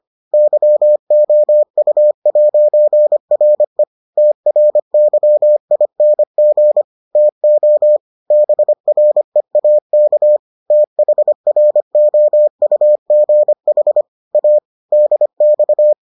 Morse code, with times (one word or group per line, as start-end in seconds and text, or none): 0.3-3.8 YOU'RE
4.2-6.8 TRYING
7.2-8.0 TO
8.3-10.4 BREAK
10.7-14.0 THROUGH
14.3-14.6 A
14.9-15.9 DX